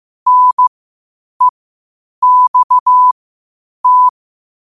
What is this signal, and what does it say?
Morse code, heard as a keyed tone, about 15 wpm, spaced as for 8 wpm, 1000 Hz. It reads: NEXT
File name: qotdfiles / 8wpm